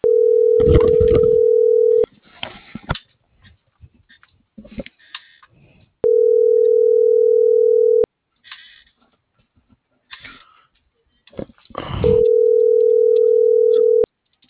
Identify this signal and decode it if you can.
Ringback tone (North America), 440 + 480 Hz.